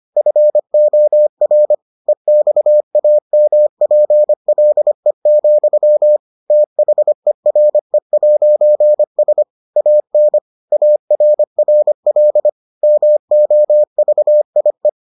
Morse code, heard as a keyed tone, about 25 wpm, 600 Hz. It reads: FOR EXAMPLE, THERE'S AN ARRL MOVIE